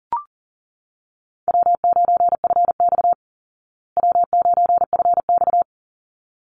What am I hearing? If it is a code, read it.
Morse code, heard as a keyed tone, about 40 wpm, 700 Hz. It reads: W9FX W9FX
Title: Easy Call Signs Repeated 2x - 40wpm